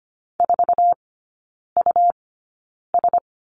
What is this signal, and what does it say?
Morse code, heard as a keyed tone, about 25 wpm, spaced as for 9 wpm, 700 Hz. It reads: 4US
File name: Lesson 17e - Letter u - Three Characters 25wpm